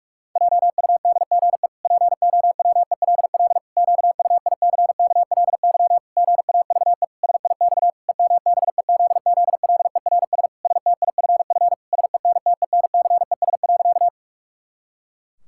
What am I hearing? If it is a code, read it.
Morse code, heard as a keyed tone, about 46 wpm, 700 Hz. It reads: JUDGE POWELL QUICKLY GAVE SIX EMBEZZLERS STIFF SENTENCES.
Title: Pangram Challenge v2 - 45wpm